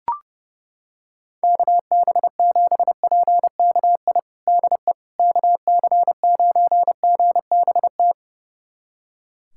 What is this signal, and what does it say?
Morse code, heard as a keyed tone, about 30 wpm, 700 Hz. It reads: KB7PKI DE KC9GBT
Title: USA Callsign DE Callsign - 6 Characters - 30wpm